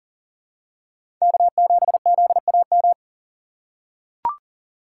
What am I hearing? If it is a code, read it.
Morse code, heard as a keyed tone, about 40 wpm, 700 Hz. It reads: K7ZAM
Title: USA Call Signs - 5 Characters - 6x Character Spacing - 40wpm